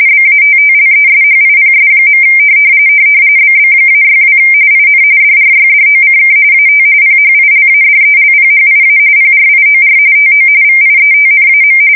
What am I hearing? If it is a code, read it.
RTTY, about 45.45 baud. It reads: BATTLESHIPS. BB-62 IN DRYDOCK AT PHILADELPHIA PRIOR TO VIETNAM
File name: fsk170 / USS-New-Jersey